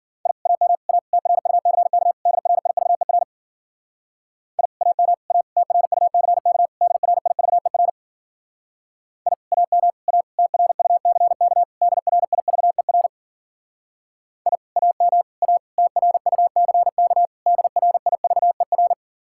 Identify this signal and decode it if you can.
Morse code, heard as a keyed tone, about 51 wpm, 700 Hz. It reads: I AM A TRUCK DRIVER I AM A TRUCK DRIVER I AM A TRUCK DRIVER I AM A TRUCK DRIVER